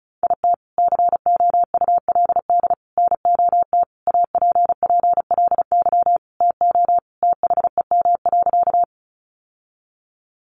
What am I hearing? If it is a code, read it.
Morse code, heard as a keyed tone, about 34 wpm, 700 Hz. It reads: IT COULD NOT APPLY TO THEM.
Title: Sentences from Top 1200 Words 35wpm